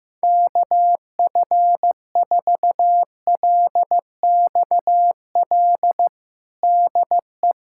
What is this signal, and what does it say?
Morse code, heard as a keyed tone, about 15 wpm, 700 Hz. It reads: KF4LXL DE
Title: USA Callsign DE Callsign - 6 Characters - 15wpm